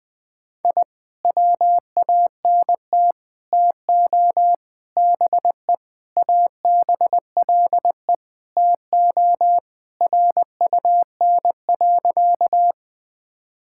Morse code, heard as a keyed tone, about 20 wpm, 700 Hz. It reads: I WANT TO BE ABLE TO RUN.
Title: Sentences with 3-Word Phrases - 20wpm